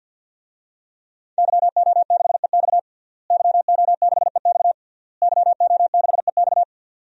Morse code, heard as a keyed tone, about 51 wpm, 700 Hz. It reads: YO6EX YO6EX YO6EX
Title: DXCC-100 Call Signs - 5 Characters - Repeated 3x - 50wpm